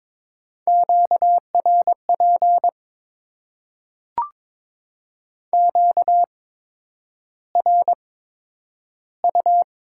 Morse code, heard as a keyed tone, about 22 wpm, spaced as for 9 wpm, 700 Hz. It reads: QRP QRU
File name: Top 90 QSO Elements - 8x Character Spacing - 22wpm